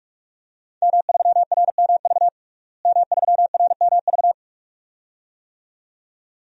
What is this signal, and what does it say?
Morse code, heard as a keyed tone, about 41 wpm, 700 Hz. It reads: M3RMV M3RMV
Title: DXCC-100 Call Signs - 5 Characters - Repeated 2x - 45wpm